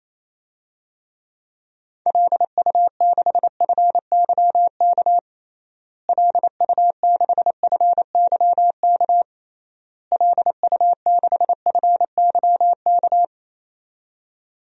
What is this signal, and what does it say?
Morse code, heard as a keyed tone, about 27 wpm, 700 Hz. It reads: LU6FYK LU6FYK LU6FYK
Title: DXCC-100 Call Signs - 6 Characters - Repeated 3x - 28wpm